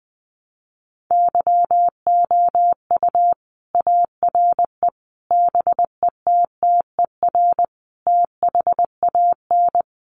Morse code, heard as a keyed tone, about 20 wpm, 700 Hz. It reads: YOU ARE BETTER THAN
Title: Sentences from Top 400 Words 20wpm